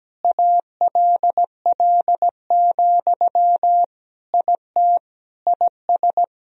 Morse code, heard as a keyed tone, about 17 wpm, 700 Hz. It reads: ALL, IT IS
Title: Sentences from Top 500 Words and Abbreviations 17wpm